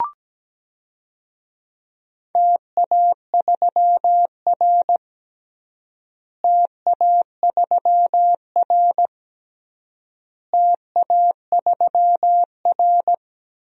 Morse code, heard as a keyed tone, about 17 wpm, 700 Hz. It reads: TA3R TA3R TA3R
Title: DXCC-100 Call Signs - 4 Characters - Repeated 3x - 17wpm